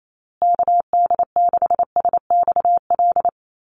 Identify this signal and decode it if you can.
Morse code, heard as a keyed tone, about 28 wpm, 700 Hz. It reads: KD6SXL
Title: USA Call Signs Repeated 2x 28wpm